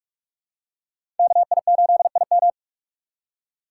Morse code, heard as a keyed tone, about 41 wpm, 700 Hz. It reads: KI8IM
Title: USA Call Signs - 5 Characters - 3x Character Spacing - 45wpm